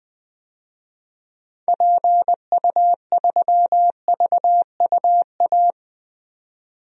Morse code, heard as a keyed tone, about 21 wpm, 700 Hz. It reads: PU3VUA
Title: DXCC-100 Call Signs - 6 Characters - 20wpm